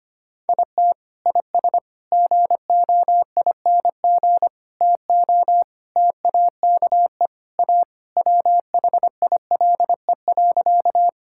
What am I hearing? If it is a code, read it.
Morse code, heard as a keyed tone, about 25 wpm, 700 Hz. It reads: IT IS GOING TO TAKE A WHILE.
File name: practice / 25wpm / Sentences with 4-Word Phrases - 25wpm